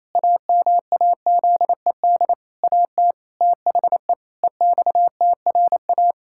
Morse code, heard as a keyed tone, about 28 wpm, 700 Hz. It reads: AMAZED AT THE EXTRA
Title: Pangram Challenge v2 - 28wpm